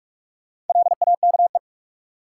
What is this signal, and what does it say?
Morse code, heard as a keyed tone, about 46 wpm, 700 Hz. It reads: LAKE
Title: Sentences from Top 1100 Words - New Word Drill Alt - 45wpm